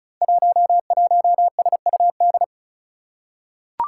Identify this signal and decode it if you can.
Morse code, heard as a keyed tone, about 36 wpm, 700 Hz. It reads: 11SUD
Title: Five Challenging Letter-Number Combinations 35wpm